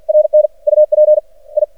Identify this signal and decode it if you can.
Morse code, heard as a keyed tone, about 53 wpm, 610 Hz. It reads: RN UP I
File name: archived / CRSnet